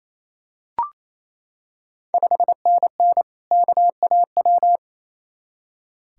Morse code, heard as a keyed tone, about 28 wpm, 690 Hz. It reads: 5NN KAW
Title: Ontario QSO Party Exchange 28wpm